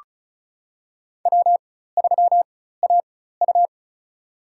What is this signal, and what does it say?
Morse code, heard as a keyed tone, about 36 wpm, spaced as for 16 wpm, 700 Hz. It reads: W3AU
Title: USA Call Signs - 4 Characters - 4x Character Spacing - 35wpm